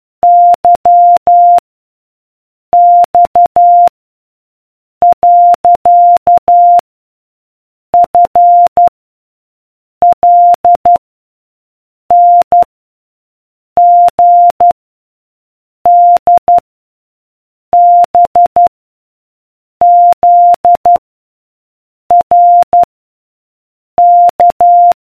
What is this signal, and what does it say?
Morse code, heard as a keyed tone, about 12 wpm, 700 Hz. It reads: Y X . F L N G D B Z R K